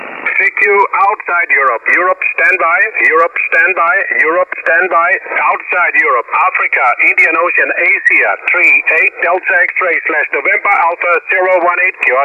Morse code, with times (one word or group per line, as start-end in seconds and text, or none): none